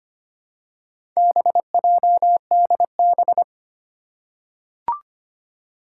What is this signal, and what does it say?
Morse code, heard as a keyed tone, about 25 wpm, 700 Hz. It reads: BJDB
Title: Four Challenging Letter-Number Combinations 25wpm